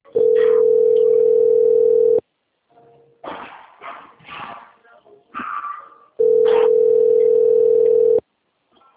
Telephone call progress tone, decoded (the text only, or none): ringback tone